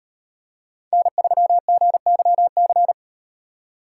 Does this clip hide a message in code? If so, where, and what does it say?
Yes: Morse code from 0.9 to 2.9 s, heard as a keyed tone, about 36 wpm, 700 Hz. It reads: N3GYC